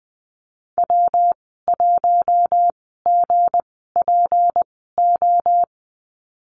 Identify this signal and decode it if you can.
Morse code, heard as a keyed tone, about 19 wpm, spaced as for 14 wpm, 700 Hz. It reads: W1GPO
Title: 179 Lesson 29g - Slash - Call Signs 20wpm